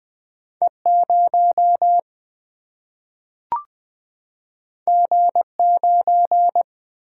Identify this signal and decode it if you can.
Morse code, heard as a keyed tone, about 20 wpm, 700 Hz. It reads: E 0 G 9
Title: Single-Letter Single-Number 20wpm